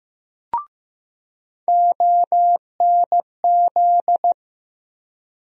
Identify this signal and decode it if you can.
Morse code, heard as a keyed tone, about 15 wpm, 700 Hz. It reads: ONZ
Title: USA Call Sign Postfixes - 3 Letters 15wpm